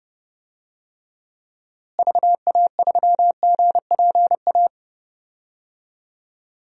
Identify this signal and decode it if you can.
Morse code, heard as a keyed tone, about 32 wpm, 700 Hz. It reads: VA3GPA
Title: DXCC-100 Call Signs - 6 Characters - Repeated 2x - 30wpm